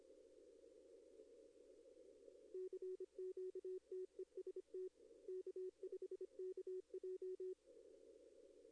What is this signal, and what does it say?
Morse code, heard as a keyed tone, about 25 wpm, 370 Hz. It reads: CQ TEST K5KJ
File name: contests / arrl-160-k5tr-op-w5tn -1300